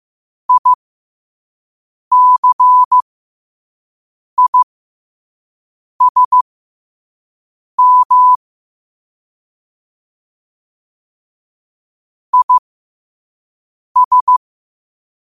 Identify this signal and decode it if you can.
Morse code, heard as a keyed tone, about 15 wpm, spaced as for 5 wpm, 1000 Hz. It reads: ICISM IS
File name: qotdfiles / 5wpm